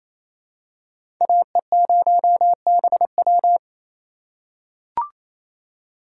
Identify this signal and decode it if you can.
Morse code, heard as a keyed tone, about 28 wpm, 700 Hz. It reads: AE0BW
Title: USA Call Signs - 3x Character Spacing - 28wpm